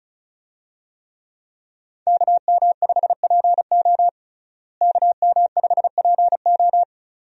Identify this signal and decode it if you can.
Morse code, heard as a keyed tone, about 34 wpm, 700 Hz. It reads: KM5PO KM5PO
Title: USA Call Signs - 5 Characters - Repeated 2x - 35wpm